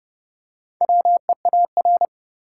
Morse code, heard as a keyed tone, about 30 wpm, 700 Hz. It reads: WEAR
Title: Four Letter Words - 30wpm